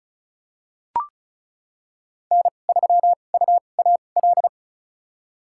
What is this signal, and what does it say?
Morse code, heard as a keyed tone, about 36 wpm, spaced as for 25 wpm, 700 Hz. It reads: N3UAL
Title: USA Call Signs - 5 Characters - 2x Character Spacing - 35wpm